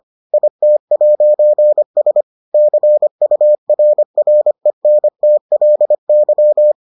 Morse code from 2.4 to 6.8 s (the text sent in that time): CURRENTLY